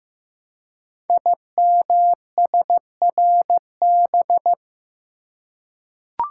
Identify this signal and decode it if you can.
Morse code, heard as a keyed tone, about 15 wpm, 700 Hz. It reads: IMSRB
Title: Code Groups - 5 Letters - 15wpm